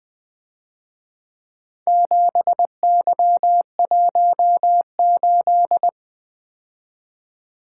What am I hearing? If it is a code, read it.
Morse code, heard as a keyed tone, about 22 wpm, 700 Hz. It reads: 7Y18
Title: Speed-Racing - Four Letter-Number Combinations - 20wpm